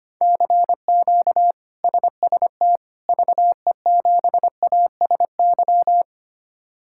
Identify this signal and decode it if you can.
Morse code, heard as a keyed tone, about 25 wpm, 700 Hz. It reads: CQ SST VE7ASY
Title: K1USN SST QSO Practice - 25wpm